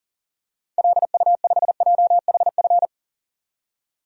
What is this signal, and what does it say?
Morse code, heard as a keyed tone, about 40 wpm, 700 Hz. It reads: LU5JHF
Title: DXCC-100 Call Signs - 6 Characters - 40wpm